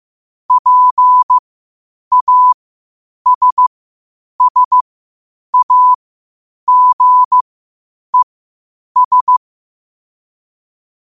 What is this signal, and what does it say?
Morse code, heard as a keyed tone, about 15 wpm, spaced as for 8 wpm, 1000 Hz. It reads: PASSAGES
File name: qotdfiles / 8wpm